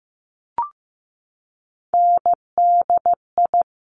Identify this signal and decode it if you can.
Morse code, heard as a keyed tone, about 15 wpm, 700 Hz. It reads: NDI
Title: Top Three Letter Combinations in Words 15wpm